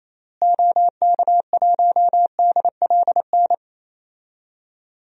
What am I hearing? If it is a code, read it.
Morse code, heard as a keyed tone, about 28 wpm, 700 Hz. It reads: OK1DLN
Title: DXCC-100 Call Signs - 6 Characters - Repeated 3x - 28wpm